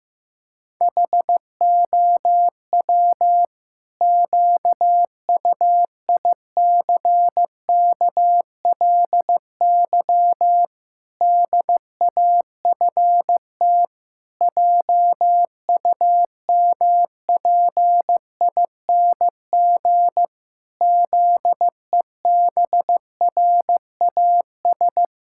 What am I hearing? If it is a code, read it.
Morse code, heard as a keyed tone, about 15 wpm, 700 Hz. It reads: HOW QUICKLY DAFT JUMPING ZEBRAS
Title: Pangram Challenge v2 - 15wpm